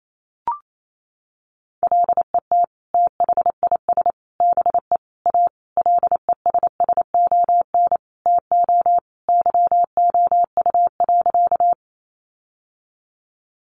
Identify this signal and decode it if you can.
Morse code, heard as a keyed tone, about 28 wpm, 700 Hz. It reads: LET THIS BE A LESSON TO YOU.
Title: Sentences from Top 1200 Words 28wpm